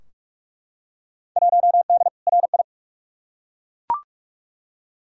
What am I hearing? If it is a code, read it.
Morse code, heard as a keyed tone, about 42 wpm, 700 Hz. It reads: 1D RI
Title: Field Day Exchanges - 45wpm